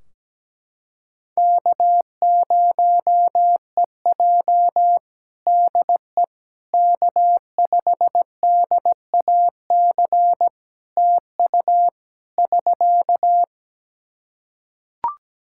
Morse code, heard as a keyed tone, about 17 wpm, 700 Hz. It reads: K0EJ DE K5DAC TU <SK>
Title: Practice QSO - Exchange 4 - Ending - 17wpm